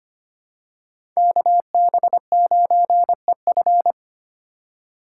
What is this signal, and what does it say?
Morse code, heard as a keyed tone, about 25 wpm, 700 Hz. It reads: KB9EF
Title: USA Call Signs - 5 Characters - 3x Character Spacing - 25wpm